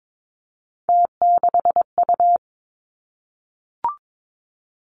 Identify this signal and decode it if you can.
Morse code, heard as a keyed tone, about 22 wpm, 700 Hz. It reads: T6U
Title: Three Challenging Letter-Number Combinations 22wpm